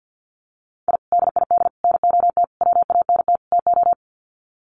Morse code, heard as a keyed tone, about 50 wpm, 700 Hz. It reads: I DID NOT WANT TO
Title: Sets of 5 Words Encore 50wpm